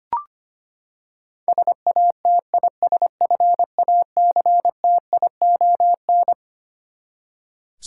Morse code, heard as a keyed tone, about 25 wpm, 700 Hz. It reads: SATISFACTION
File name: Twelve Letter Words - 25wpm